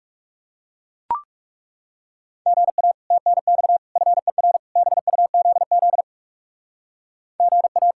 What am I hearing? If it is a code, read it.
Morse code, heard as a keyed tone, about 36 wpm, 700 Hz. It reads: GA TNX FER BUZZ GA
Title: Speed-Racing Common QSO Phrases - 30wpm